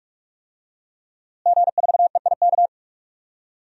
Morse code, heard as a keyed tone, about 42 wpm, 700 Hz. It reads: G4EIK
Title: DXCC-100 Call Signs - 5 Characters - Repeated 2x - 45wpm